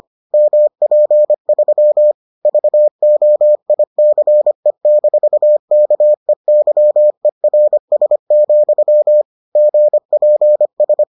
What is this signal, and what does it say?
Morse code, heard as a keyed tone, about 25 wpm, 600 Hz. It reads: MP3 VOICE-KEYERS, GPS